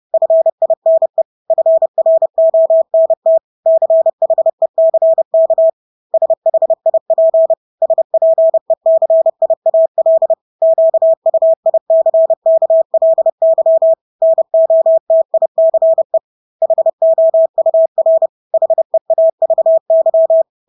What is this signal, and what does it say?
Morse code, heard as a keyed tone, about 30 wpm, 650 Hz. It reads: FINE FRONT CHECK SHIP SPECIAL QUICKLY NOTICE HOUR HEAVY
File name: Engelske ord 30-30 wpm